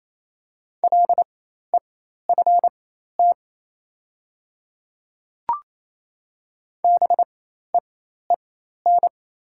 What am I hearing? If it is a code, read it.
Morse code, heard as a keyed tone, about 28 wpm, spaced as for 13 wpm, 700 Hz. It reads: LEFT BEEN
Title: 4x Character Spacing - Top 300 Words - 28wpm